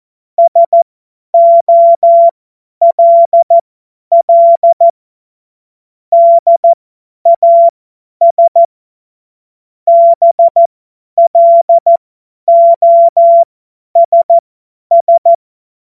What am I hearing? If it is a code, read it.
Morse code, heard as a keyed tone, about 14 wpm, spaced as for 10 wpm, 680 Hz. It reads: SOLL DAS BLOSS
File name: Morsekurs / Witze / Witz 05 März 2024 50 F70